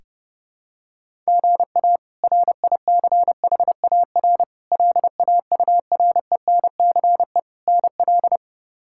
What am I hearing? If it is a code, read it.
Morse code, heard as a keyed tone, about 30 wpm, 700 Hz. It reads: GA RICHAR LAURENCE NL
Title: K1USN SST QSO Practice - 30wpm